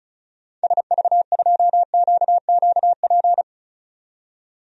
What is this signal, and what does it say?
Morse code, heard as a keyed tone, about 36 wpm, 700 Hz. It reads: SV2QQP
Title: DXCC-100 Call Signs - 6 Characters - 35wpm